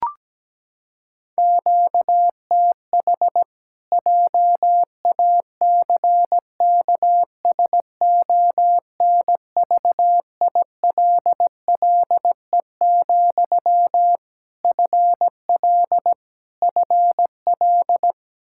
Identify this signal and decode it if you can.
Morse code, heard as a keyed tone, about 17 wpm, 700 Hz. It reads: QTH JACKSONVILLE, FL FL
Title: Practice QSO - Exchange 1 - Beginning - 17wpm